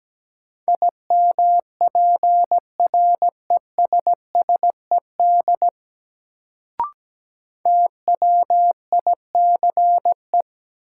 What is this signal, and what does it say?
Morse code, heard as a keyed tone, about 17 wpm, 700 Hz. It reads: IMPRESSED TWICE
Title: Sentences from Top 1600 Words - New Word Drill - 17wpm